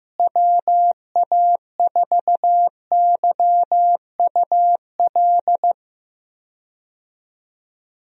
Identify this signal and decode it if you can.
Morse code, heard as a keyed tone, about 15 wpm, 700 Hz. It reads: WA4YUL
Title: USA Call Signs - 6 Characters - Repeated 3x - 15wpm